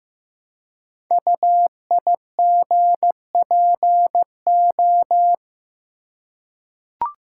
Morse code, heard as a keyed tone, about 15 wpm, 700 Hz. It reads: UIGPO
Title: Code Groups - 5 Letters - 15wpm